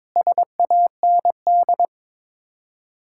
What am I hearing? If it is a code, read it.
Morse code, heard as a keyed tone, about 22 wpm, 700 Hz. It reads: SAND
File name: practice / 22wpm / Sets of 7 Words - 22wpm